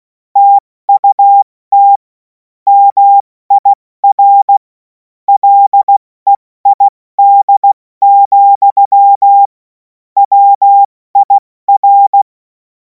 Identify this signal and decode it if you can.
Morse code, heard as a keyed tone, about 16 wpm, 800 Hz. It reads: TUT MIR LEID, WIR
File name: Morsekurs / Witze / Witz 16. Juni 2026 70 F80